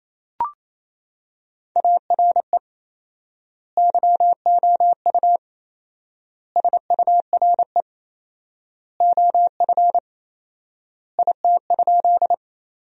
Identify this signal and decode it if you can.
Morse code, heard as a keyed tone, about 28 wpm, 700 Hz. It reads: ARE YOU SURE OF IT?
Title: 4x Word-Spacing - Sentences from Top 300 Words - 28wpm